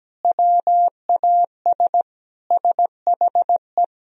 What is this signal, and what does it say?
Morse code, heard as a keyed tone, about 17 wpm, 700 Hz. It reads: WAS SHE
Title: Sentences from Top 400 Words and Abbreviations 17wpm